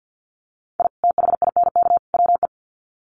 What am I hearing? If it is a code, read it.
Morse code, heard as a keyed tone, about 50 wpm, 700 Hz. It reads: I THINK WE
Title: Sets of 3 Words 50wpm